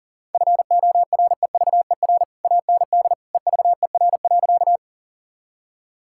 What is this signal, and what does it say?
Morse code, heard as a keyed tone, about 37 wpm, 700 Hz. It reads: FOREVER AND EVER.
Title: Sentences from Top 1600 Words 40wpm